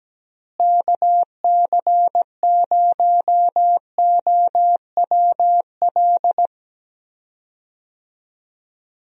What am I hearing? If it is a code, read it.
Morse code, heard as a keyed tone, about 17 wpm, 700 Hz. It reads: KC0OWL
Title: USA Call Signs - 6 Characters - Repeated 3x - 17wpm